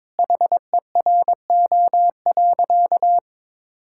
Morse code, heard as a keyed tone, about 22 wpm, 700 Hz. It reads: HERO.
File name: practice / 22wpm / Sentences with 4-Word Phrases - 22wpm